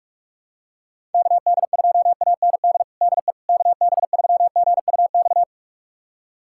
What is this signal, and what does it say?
Morse code, heard as a keyed tone, about 46 wpm, 700 Hz. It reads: KD2AND DE KB3GUX
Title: USA Callsign DE Callsign - 6 Characters - 45wpm